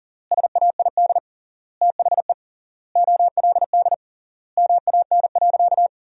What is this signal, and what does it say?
Morse code, heard as a keyed tone, about 37 wpm, 700 Hz. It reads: SAID THE OLD MAN.
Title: 3x Word-Spacing - Sentences from Top 200 Words - 40wpm